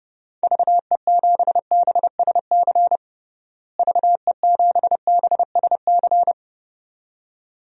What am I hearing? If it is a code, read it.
Morse code, heard as a keyed tone, about 30 wpm, 700 Hz. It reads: VE7BSC VE7BSC
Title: DXCC-100 Call Signs - 6 Characters - Repeated 2x - 30wpm